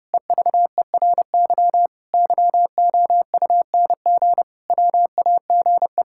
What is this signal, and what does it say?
Morse code, heard as a keyed tone, about 32 wpm, 700 Hz. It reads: EVERY YOUNG WAGE
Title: Pangram Challenge v2 - 30wpm